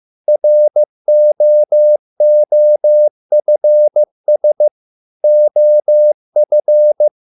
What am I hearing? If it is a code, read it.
Morse code, heard as a keyed tone, about 15 wpm, 600 Hz. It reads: ROOFS OF